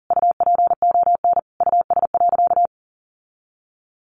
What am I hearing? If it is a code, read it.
Morse code, heard as a keyed tone, about 40 wpm, 700 Hz. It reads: UPON US.
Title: Sentences from Top 1400 Words 40wpm